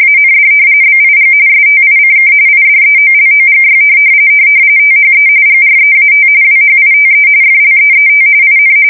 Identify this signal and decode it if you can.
RTTY, about 45.45 baud. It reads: AND BIG STICK IDEOLOGY THE GREAT WHITE FLEET IN
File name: fsk170 / usnavy3